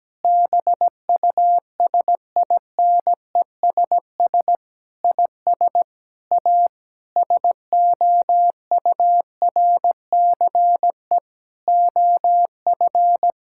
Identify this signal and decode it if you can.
Morse code, heard as a keyed tone, about 17 wpm, 700 Hz. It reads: BUSINESS IS A SOURCE OF
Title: Sentences from Top 1000 Words - Review Part 6 - 17wpm